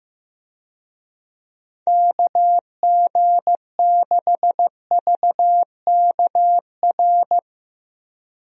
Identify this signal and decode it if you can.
Morse code, heard as a keyed tone, about 15 wpm, 700 Hz. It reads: KG6VKR
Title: USA Call Signs - 6 Characters - Repeated 3x - 15wpm